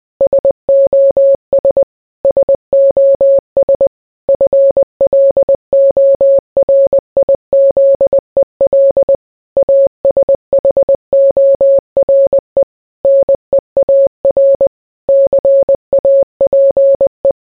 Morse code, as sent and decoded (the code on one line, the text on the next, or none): ... --- ... / ... --- ... / ..-. .-.. --- .-. .. --.. .-.. / .- ... .... --- .-. / -. .- .-. / -.-. .- .--. .
SOS SOS FLORIZEL ASHORE NEAR CAPE